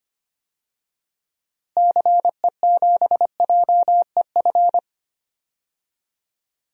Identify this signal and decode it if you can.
Morse code, heard as a keyed tone, about 25 wpm, 700 Hz. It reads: CE7JEF
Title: DXCC-100 Call Signs - 6 Characters - Repeated 2x - 25wpm